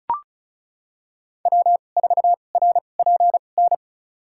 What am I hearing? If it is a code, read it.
Morse code, heard as a keyed tone, about 34 wpm, spaced as for 25 wpm, 700 Hz. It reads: W4RPN
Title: USA Call Signs - 2x Character Spacing - 35wpm